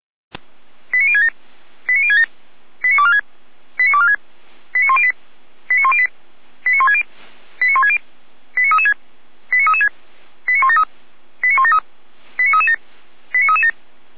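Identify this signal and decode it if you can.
Five-tone selective call (ZVEI), heard as quick runs of five tones, tones of about 70 ms. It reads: 89067 89067 89246 89246 89108 89108 89160 89160 89336 89336 89173 89173 89337 89337
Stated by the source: ZVEI five-tone selective call received on SDR.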